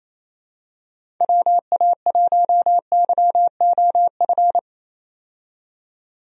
Morse code, heard as a keyed tone, about 28 wpm, 700 Hz. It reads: WA1YOF
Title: USA Call Signs - 6 Characters - 28wpm